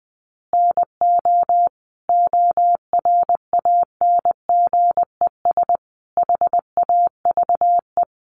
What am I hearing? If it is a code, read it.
Morse code, heard as a keyed tone, about 20 wpm, 700 Hz. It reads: NO ORANGES HAVE
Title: Easy Phrases 20wpm